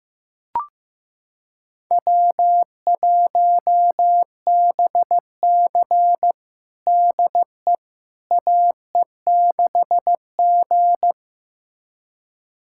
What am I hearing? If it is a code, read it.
Morse code, heard as a keyed tone, about 16 wpm, 700 Hz. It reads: W1BC DE AE6G
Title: USA Callsign DE Callsign - 4 Characters - 15wpm